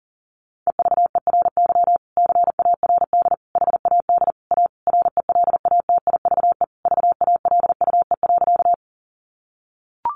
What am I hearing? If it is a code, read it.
Morse code, heard as a keyed tone, about 40 wpm, 700 Hz. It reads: EVERY CARD HAD A RELATIVE VALUE.
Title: Sentences from Top 1400 Words 40wpm